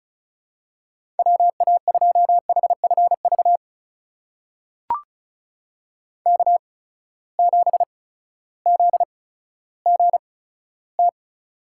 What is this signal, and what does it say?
Morse code, heard as a keyed tone, about 36 wpm, spaced as for 16 wpm, 700 Hz. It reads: WA2HFV K7ZGT
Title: USA Call Signs - 8x Character Spacing - 35wpm